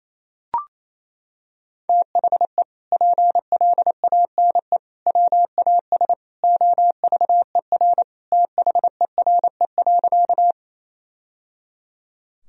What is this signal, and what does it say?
Morse code, heard as a keyed tone, about 27 wpm, 700 Hz. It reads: THE PLANE WAS OVER THERE.
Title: Sentences from Top 1000 Words - Review Part 3 - 28wpm